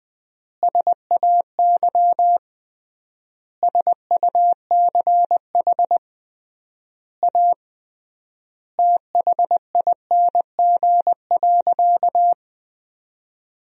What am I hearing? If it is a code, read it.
Morse code, heard as a keyed tone, about 20 wpm, 700 Hz. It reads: SAY SUCH A THING.
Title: 3x Word-Spacing - Sentences from Top 200 Words - 20wpm